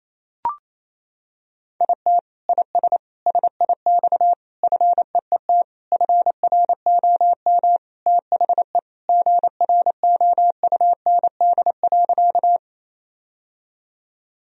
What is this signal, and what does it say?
Morse code, heard as a keyed tone, about 27 wpm, 700 Hz. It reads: IT IS SIX FEET FROM THE GROUND.
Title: Sentences from Top 1000 Words - Review Part 2 - 28wpm